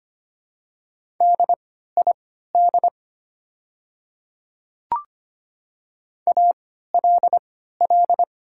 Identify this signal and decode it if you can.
Morse code, heard as a keyed tone, about 25 wpm, spaced as for 14 wpm, 700 Hz. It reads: DID ALL
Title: 3x Character Spacing - Top 100 Words - 25wpm